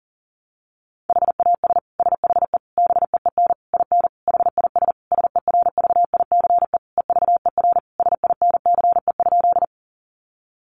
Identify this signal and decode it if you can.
Morse code, heard as a keyed tone, about 40 wpm, 700 Hz. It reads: HAS SHE BEEN IN HIS SERVICE EVER SINCE?
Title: Sentences from Top 700 Words 40wpm